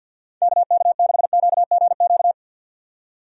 Morse code, heard as a keyed tone, about 51 wpm, 700 Hz. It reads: KK6QGQ
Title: USA Call Signs - 6 Characters - 50wpm